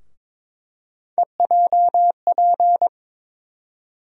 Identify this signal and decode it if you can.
Morse code, heard as a keyed tone, about 22 wpm, 700 Hz. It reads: EJP
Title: USA Call Sign Postfixes - 3 Letters 22wpm